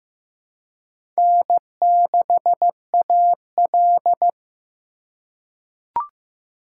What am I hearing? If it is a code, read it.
Morse code, heard as a keyed tone, about 15 wpm, 700 Hz. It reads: N6AL
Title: USA Call Signs - 2x Character Spacing - 15wpm